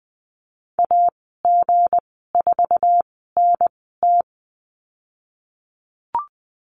Morse code, heard as a keyed tone, about 20 wpm, spaced as for 15 wpm, 700 Hz. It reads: AG4NT
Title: 167 Lesson 27g - Number 7 - Call Signs 20wpm